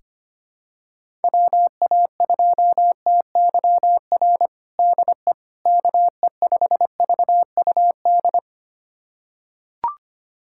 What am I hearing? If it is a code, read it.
Morse code, heard as a keyed tone, about 25 wpm, 700 Hz. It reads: WA2TYR DE KE5VUD
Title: USA Callsign DE Callsign - 6 Characters - 25wpm